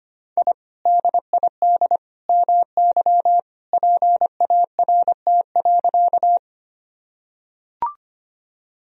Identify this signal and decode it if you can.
Morse code, heard as a keyed tone, about 25 wpm, 700 Hz. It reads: I DID MY PART.
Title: Sentences from Top 1000 Words - Review Part 1 - 25wpm